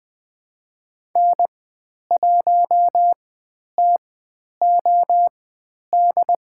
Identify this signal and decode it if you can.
Morse code, heard as a keyed tone, about 20 wpm, spaced as for 10 wpm, 700 Hz. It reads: N1TOD
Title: 095 Lesson 16g - Letter c - Call Signs 20wpm